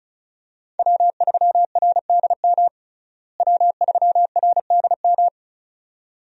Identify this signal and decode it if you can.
Morse code, heard as a keyed tone, about 34 wpm, 700 Hz. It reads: W3RDM W3RDM
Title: USA Call Signs - 5 Characters - Repeated 2x - 35wpm